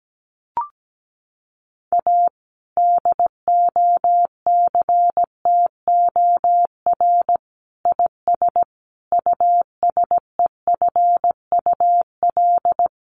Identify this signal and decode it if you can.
Morse code, heard as a keyed tone, about 17 wpm, 700 Hz. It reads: A DOCTOR IS USEFUL
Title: Sentences from Top 1500 Words 17wpm